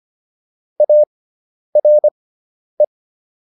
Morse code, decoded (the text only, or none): ARE